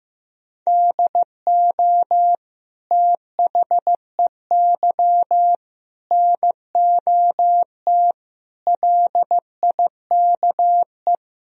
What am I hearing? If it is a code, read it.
Morse code, heard as a keyed tone, about 16 wpm, 700 Hz. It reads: DO THEY NOT LIKE
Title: Sentences from Top 1000 Words - Review Part 4 - 15wpm